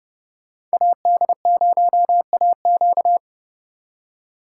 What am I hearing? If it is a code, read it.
Morse code, heard as a keyed tone, about 36 wpm, 700 Hz. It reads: AD0AQ
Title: Speed Racing - USA 2x2 Call Signs 30wpm